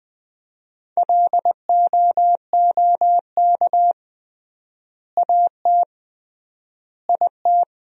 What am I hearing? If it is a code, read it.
Morse code, heard as a keyed tone, about 20 wpm, 700 Hz. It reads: LOOK AT IT
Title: 3x Word-Spacing - Sentences from Top 100 Words - 20wpm